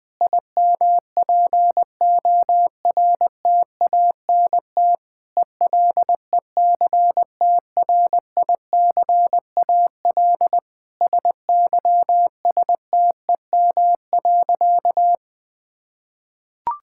Morse code, heard as a keyed tone, about 20 wpm, 700 Hz. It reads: IMPORTANT ELECTRICAL SYSTEM.
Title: Sentences from Top 1000 Words - Review Part 6 - 20wpm